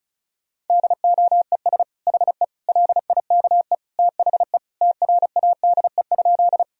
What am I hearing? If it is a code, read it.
Morse code, heard as a keyed tone, about 34 wpm, 700 Hz. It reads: DOES HE LIKE THE TRADE?
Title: Sentences from Top 1000 Words - Review Part 4 - 35wpm